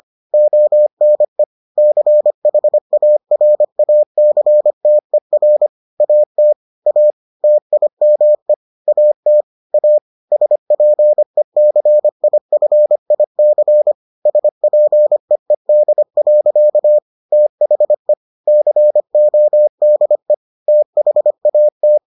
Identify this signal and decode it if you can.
Morse code, heard as a keyed tone, about 25 wpm, 600 Hz. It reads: ONE CHARACTER AT A TIME AT A SPECIFIC SPEED. THE CODE THAT